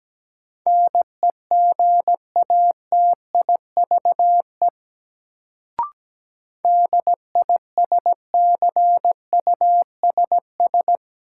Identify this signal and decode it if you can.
Morse code, heard as a keyed tone, about 17 wpm, 700 Hz. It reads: NEGATIVE DISCUSS
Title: Sentences from Top 1000 Words - New Word Drill - 17wpm